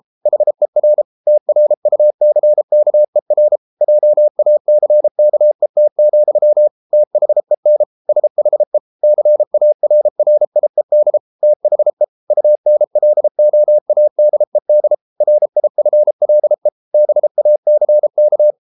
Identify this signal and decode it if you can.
Morse code, heard as a keyed tone, about 33 wpm, 600 Hz. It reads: HER TRUCKER JACKET, THEN SHE CARRIED THE UNLOADED RIFLE BACK